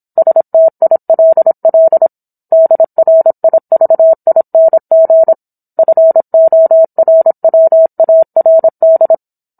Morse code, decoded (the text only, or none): STILL DRIVING FORWARD